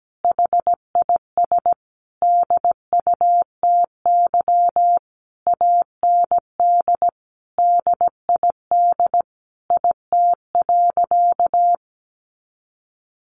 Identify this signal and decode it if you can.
Morse code, heard as a keyed tone, about 17 wpm, 700 Hz. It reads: HIS DUTY AND DID IT.
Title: Sentences from Top 1500 Words 17wpm